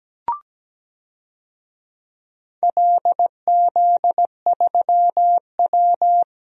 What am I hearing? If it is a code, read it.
Morse code, heard as a keyed tone, about 17 wpm, 700 Hz. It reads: LZ3W
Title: DXCC-100 Call Signs - 4 Characters - Repeated 3x - 17wpm